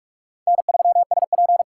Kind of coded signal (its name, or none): Morse code